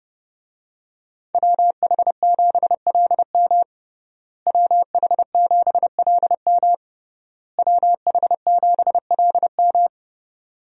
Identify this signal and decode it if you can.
Morse code, heard as a keyed tone, about 28 wpm, 700 Hz. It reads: WH7LM WH7LM WH7LM
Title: USA Call Signs - 5 Characters - Repeated 3x - 30wpm